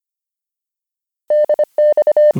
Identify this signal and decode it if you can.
Morse code, heard as a keyed tone, about 25 wpm, 600 Hz. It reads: DX